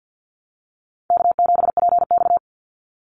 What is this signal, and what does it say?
Morse code, heard as a keyed tone, about 50 wpm, 700 Hz. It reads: K7PX
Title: NA SOTA Activator Callsigns 50wpm